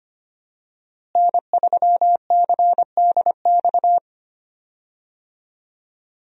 Morse code, heard as a keyed tone, about 25 wpm, 700 Hz. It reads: N3CDX
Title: USA Call Signs - 5 Characters - Repeated 2x - 25wpm